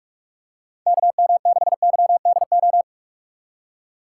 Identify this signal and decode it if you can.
Morse code, heard as a keyed tone, about 46 wpm, 700 Hz. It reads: KM6YDO
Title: USA Call Signs - 6 Characters - 3x Character Spacing - 45wpm